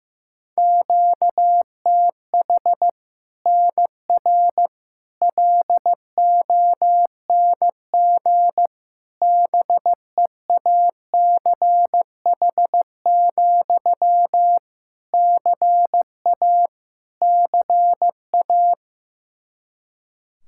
Morse code, heard as a keyed tone, about 15 wpm, 700 Hz. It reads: QTH NR LONG BEACH, CA CA
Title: K1USN SST QSO Practice 15wpm